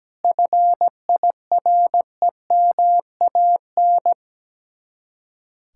Morse code, heard as a keyed tone, about 17 wpm, 700 Hz. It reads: FIREMAN
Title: Speed-Racing Common QSO Phrases Encore - 17wpm